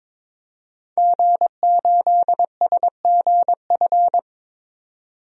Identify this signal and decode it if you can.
Morse code, heard as a keyed tone, about 21 wpm, 700 Hz. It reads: G8SGF
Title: DXCC-100 Call Signs - 5 Characters - 22wpm